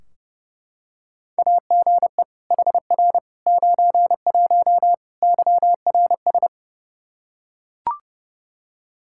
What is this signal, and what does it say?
Morse code, heard as a keyed tone, about 32 wpm, 700 Hz. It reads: AGE HR 91 YRS
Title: Practice QSO - Exchange 3 - More Info - 30wpm